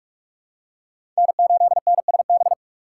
Morse code, heard as a keyed tone, about 47 wpm, 700 Hz. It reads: N8NSB
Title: USA Call Signs - 5 Characters - 2x Character Spacing - 45wpm